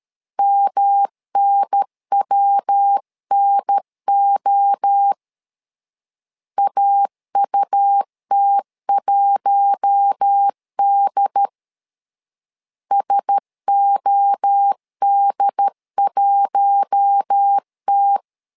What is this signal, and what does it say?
Morse code, heard as a keyed tone, about 13 wpm, 800 Hz. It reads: MNWNO AUT1D SOD1T
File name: LEZIONI / 4 Q1S 12wpm 1802